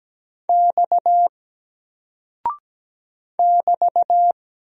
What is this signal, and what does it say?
Morse code, heard as a keyed tone, about 17 wpm, 700 Hz. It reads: X <BT>
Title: Single Character (Letter:Number:Punctuation:Prosign) 17wpm